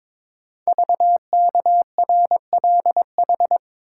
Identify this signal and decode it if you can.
Morse code, heard as a keyed tone, about 22 wpm, 700 Hz. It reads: VKRLH
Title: Code Groups - 5 Letters - 22wpm